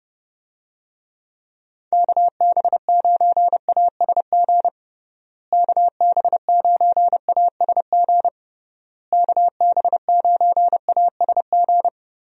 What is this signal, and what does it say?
Morse code, heard as a keyed tone, about 30 wpm, 700 Hz. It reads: KB9ASG KB9ASG KB9ASG
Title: USA Call Signs - 6 Characters - Repeated 3x - 30wpm